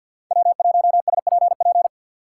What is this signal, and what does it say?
Morse code, heard as a keyed tone, about 50 wpm, 690 Hz. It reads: W1SPP
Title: USA Call Signs - 5 Characters - 3x Character Spacing - 50wpm